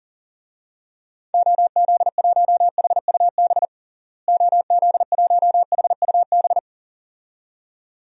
Morse code, heard as a keyed tone, about 40 wpm, 700 Hz. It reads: OZ1HUB OZ1HUB
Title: DXCC-100 Call Signs - 6 Characters - Repeated 2x - 40wpm